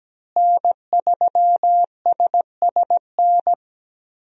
Morse code, heard as a keyed tone, about 17 wpm, 700 Hz. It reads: N3SSN
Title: USA Call Signs - 5 Characters - Repeated 3x - 17wpm